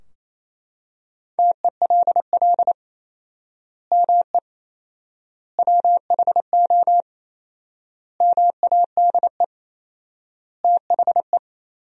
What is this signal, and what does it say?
Morse code, heard as a keyed tone, about 28 wpm, 700 Hz. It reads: TELL ME WHO MADE THE
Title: 4x Word-Spacing - Sentences from Top 200 Words - 28wpm